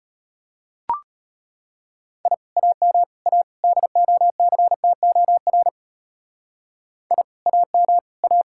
Morse code, heard as a keyed tone, about 30 wpm, 700 Hz. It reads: I AM A DOCTOR I AM A
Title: Speed-Racing Common QSO Phrases - 25wpm